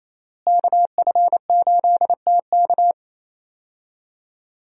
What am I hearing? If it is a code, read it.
Morse code, heard as a keyed tone, about 28 wpm, 700 Hz. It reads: KF8TK
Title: USA Call Signs - 5 Characters - Repeated 3x - 28wpm